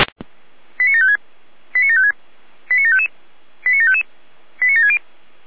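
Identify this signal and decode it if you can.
Five-tone selective call (ZVEI), heard as quick runs of five tones, tones of about 70 ms. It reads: 89756 89756 89755 89755 89760
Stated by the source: ZVEI five-tone selective call 89730, received by SDR.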